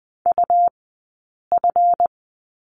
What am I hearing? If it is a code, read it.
Morse code, heard as a keyed tone, about 20 wpm, 700 Hz. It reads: U F
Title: 137 Lesson 23d - Letter f - Two Characters 20wpm